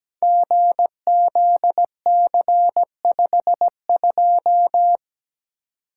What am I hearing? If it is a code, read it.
Morse code, heard as a keyed tone, about 17 wpm, 700 Hz. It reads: GZC52
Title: Code Groups - 5 Letter:Numbers - 17wpm